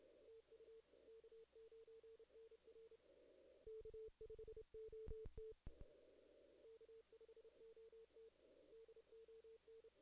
Morse code, heard as a keyed tone, about 31 wpm, 440 Hz. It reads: TU K9NR K5OT K5OT DON
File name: 2016 JAN CW NAQP / 2016 Jan naqp k5ot - 0-11